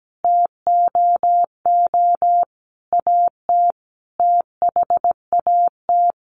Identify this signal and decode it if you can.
Morse code, heard as a keyed tone, about 17 wpm, 700 Hz. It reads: TOO AT THAT
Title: Sentences from Top 1200 Words 17wpm